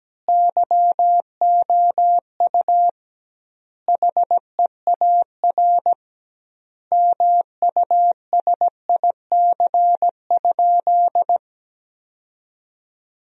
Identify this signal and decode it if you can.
Morse code, heard as a keyed tone, about 17 wpm, 700 Hz. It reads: YOU HEAR MUSIC?
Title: 2x Word-Spacing - Sentences from Top 300 Words - 17wpm